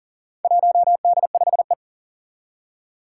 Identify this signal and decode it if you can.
Morse code, heard as a keyed tone, about 39 wpm, 700 Hz. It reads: 1D5E
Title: Four Challenging Letter-Number Combinations 40wpm